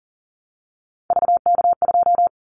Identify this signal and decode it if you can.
Morse code, heard as a keyed tone, about 41 wpm, 700 Hz. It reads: VK2
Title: Callsign Prefixes for Top 50 DXCC - 40wpm